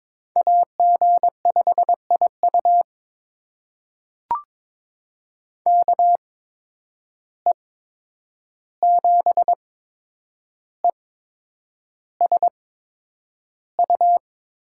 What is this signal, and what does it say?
Morse code, heard as a keyed tone, about 22 wpm, spaced as for 9 wpm, 700 Hz. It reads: AG5IU KE7ESU
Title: USA Call Signs - 8x Character Spacing - 22wpm